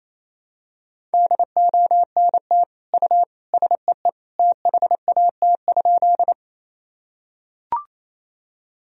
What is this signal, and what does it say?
Morse code, heard as a keyed tone, about 28 wpm, 700 Hz. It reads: DONT U SEE THAT?
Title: Sentences from Top 300 Words and Abbreviations 28wpm